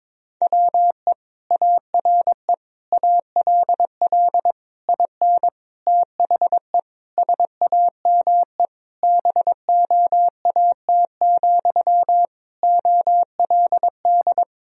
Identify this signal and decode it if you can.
Morse code, heard as a keyed tone, about 22 wpm, 700 Hz. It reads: WE ARE ALL IN THE SAME BOAT, OLD
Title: Sentences from Top 1000 Words - Review Part 3 - 22wpm